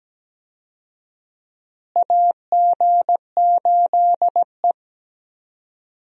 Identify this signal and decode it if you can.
Morse code, heard as a keyed tone, about 17 wpm, 700 Hz. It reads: AG8E
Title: USA Call Signs - 4 Characters - Repeated 2x - 17wpm